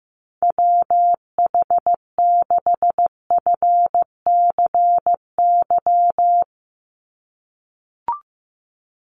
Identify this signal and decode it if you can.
Morse code, heard as a keyed tone, about 15 wpm, 700 Hz. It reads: WH6FCY
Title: USA Call Signs 15wpm